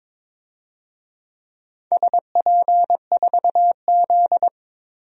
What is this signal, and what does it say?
Morse code, heard as a keyed tone, about 21 wpm, 700 Hz. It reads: SP4Z
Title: DXCC-100 Call Signs - 4 Characters - 22wpm